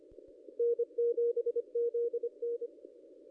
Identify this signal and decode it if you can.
Morse code, heard as a keyed tone, about 25 wpm, 470 Hz. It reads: N7ZN
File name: contests / ss-cw-k5tr-op-n5rz -1800